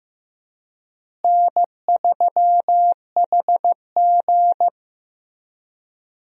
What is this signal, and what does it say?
Morse code, heard as a keyed tone, about 15 wpm, 700 Hz. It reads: N3HG
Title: USA Call Signs - 4 Characters - Repeated 3x - 15wpm